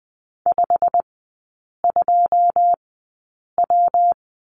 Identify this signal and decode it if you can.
Morse code, heard as a keyed tone, about 20 wpm, spaced as for 8 wpm, 700 Hz. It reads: 52W
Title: 121 Lesson 20e - Number 3 - Three Characters 20wpm